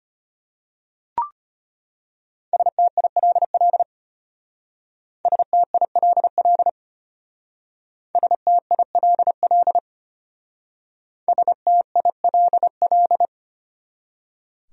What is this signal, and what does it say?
Morse code, heard as a keyed tone, about 32 wpm, 700 Hz. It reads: STILL STILL STILL STILL